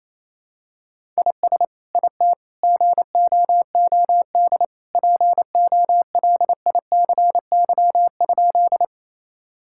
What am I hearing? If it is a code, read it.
Morse code, heard as a keyed tone, about 28 wpm, 700 Hz. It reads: IS IT GOOD POLICY?
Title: Sentences from Top 1000 Words - Review Part 5 - 28wpm